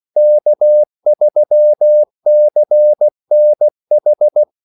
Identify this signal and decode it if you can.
Morse code, heard as a keyed tone, about 16 wpm, 600 Hz. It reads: K3CNH
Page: Kaldesignaler 16 wpm